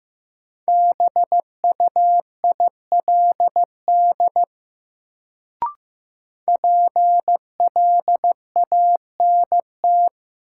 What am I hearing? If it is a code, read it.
Morse code, heard as a keyed tone, about 15 wpm, 700 Hz. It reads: BUILD PLANT
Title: Sentences from Top 1000 Words - Review Part 1 - New Word Drill - 15wpm